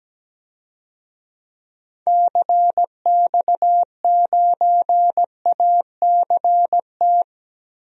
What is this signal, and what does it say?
Morse code, heard as a keyed tone, about 17 wpm, 700 Hz. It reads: CX9ACT
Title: DXCC-100 Call Signs - 6 Characters - 17wpm